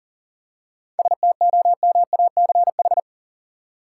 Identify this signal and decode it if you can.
Morse code, heard as a keyed tone, about 40 wpm, 700 Hz. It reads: STOMACH
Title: Seven Letter Words - 40wpm